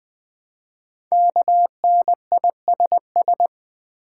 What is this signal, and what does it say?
Morse code, heard as a keyed tone, about 20 wpm, 700 Hz. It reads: KNISS
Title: Code Groups - 5 Letters - 20wpm